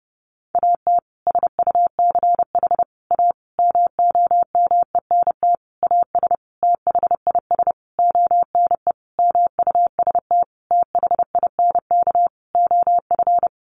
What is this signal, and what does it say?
Morse code, heard as a keyed tone, about 30 wpm, 700 Hz. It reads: AT SUCH A MOMENT AS THIS ONE MUST THINK OF
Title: Sentences from Top 1000 Words 30wpm